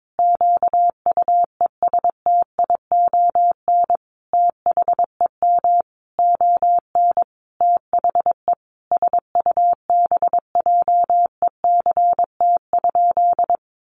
Morse code, heard as a keyed tone, about 22 wpm, 700 Hz. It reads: QUESTION THEM ON THE SUBJECT?
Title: Sentences from Top 800 Words 22wpm